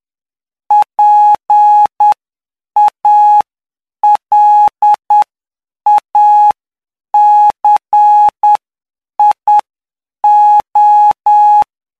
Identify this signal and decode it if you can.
Morse code, heard as a keyed tone, about 9 wpm, spaced as for 7 wpm, 840 Hz. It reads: PALACIO
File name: cw / py2lcd / 07-PY2LCD-Li��o 16